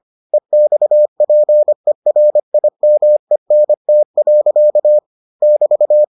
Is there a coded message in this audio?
Yes: Morse code from 0.3 to 6.0 s, heard as a keyed tone, about 25 wpm, 600 Hz. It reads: EXPERIMENT. <BT>